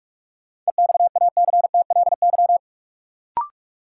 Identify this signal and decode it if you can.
Morse code, heard as a keyed tone, about 43 wpm, 700 Hz. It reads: EXACTLY
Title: Sentences from Top 900 Words - New Word Drill - 45wpm